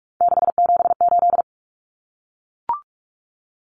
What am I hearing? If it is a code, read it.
Morse code, heard as a keyed tone, about 44 wpm, 700 Hz. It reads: 678
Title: Numbers - 3-Digit 45wpm